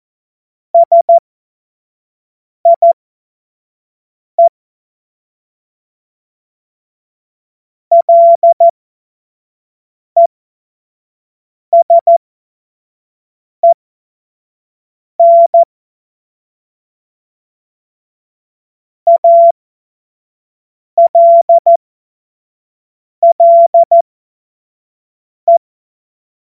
Morse code, heard as a keyed tone, about 14 wpm, spaced as for 5 wpm, 680 Hz. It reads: SIE LESEN ALLE
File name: Morsekurs / Witze / Witz 20. Juli 2021, 25 F70